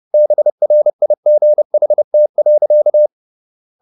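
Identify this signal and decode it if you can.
Morse code, heard as a keyed tone, about 30 wpm, 600 Hz. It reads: BRIGHT.
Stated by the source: Morse code at 30 wpm, the novel Never by Ken Follett in English, from CW med Gnister.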